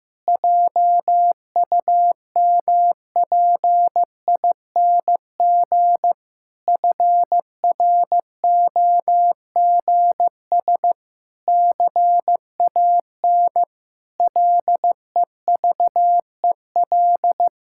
Morse code, heard as a keyed tone, about 16 wpm, 700 Hz. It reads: JUMPING FROGS CAN LEVEL
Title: Pangram Challenge v2 - 15wpm